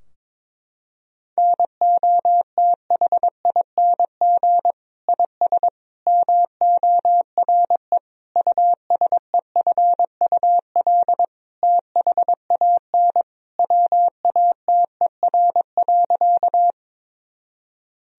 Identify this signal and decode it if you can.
Morse code, heard as a keyed tone, about 21 wpm, 700 Hz. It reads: NOTHING IS MORE USEFUL THAN WATER.
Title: Sentences from Top 1000 Words - Review Part 4 - 22wpm